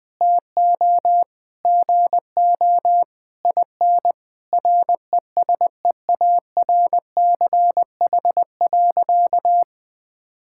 Morse code, heard as a keyed tone, about 20 wpm, 700 Hz. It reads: TO GO IN RESEARCH.
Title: Sentences with 9-Word Phrases - 20wpm